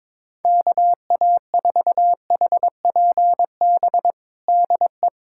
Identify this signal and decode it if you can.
Morse code, heard as a keyed tone, about 23 wpm, 700 Hz. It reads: KA4HPB DE
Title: USA Callsign DE Callsign 22wpm